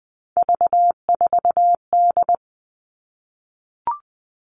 Morse code, heard as a keyed tone, about 20 wpm, 700 Hz. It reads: V4D
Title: All Cut Numbers - 3-Digit 20wpm